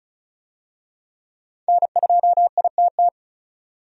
Morse code, heard as a keyed tone, about 36 wpm, 700 Hz. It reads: N2ITT
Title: USA Call Signs - 5 Characters - 35wpm